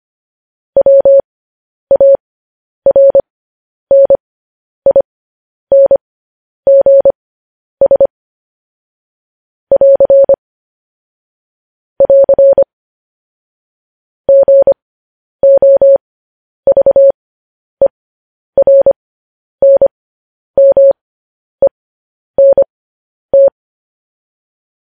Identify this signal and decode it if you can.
Morse code, heard as a keyed tone, about 25 wpm, spaced as for 10 wpm, 550 Hz. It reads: WARNINGS <AR> <AR> GOVERNMENT
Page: news headlines in morse code at 10 wpm